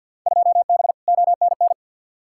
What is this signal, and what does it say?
Morse code, heard as a keyed tone, about 47 wpm, 700 Hz. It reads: 2B ONN
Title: Field Day Exchanges - 50wpm